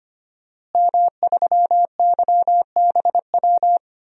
Morse code, heard as a keyed tone, about 25 wpm, 700 Hz. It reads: M3YBW
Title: DXCC-100 Call Signs - 5 Characters - Repeated 3x - 25wpm